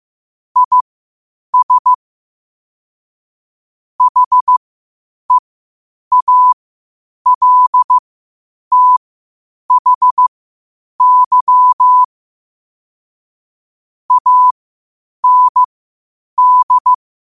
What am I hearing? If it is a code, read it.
Morse code, heard as a keyed tone, about 15 wpm, spaced as for 8 wpm, 1000 Hz. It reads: IS HEALTHY AND
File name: qotdfiles / 8wpm